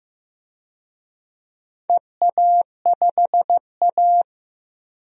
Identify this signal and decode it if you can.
Morse code, heard as a keyed tone, about 15 wpm, 700 Hz. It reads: EA5A
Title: DXCC-100 Call Signs - 4 Characters - 15wpm